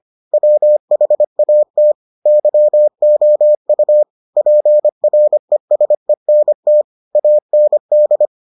Morse code, heard as a keyed tone, about 25 wpm, 600 Hz. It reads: WHAT YOU PRESENT AND